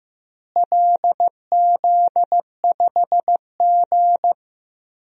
Morse code, heard as a keyed tone, about 15 wpm, 700 Hz. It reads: LZ5G
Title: DXCC-100 Call Signs - 4 Characters - Repeated 3x - 15wpm